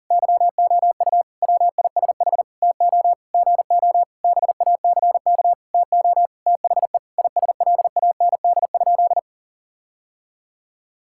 Morse code, heard as a keyed tone, about 37 wpm, 700 Hz. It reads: YOU WISH TO GO BACK TO THE ISLAND?
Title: Sentences from Top 1600 Words 40wpm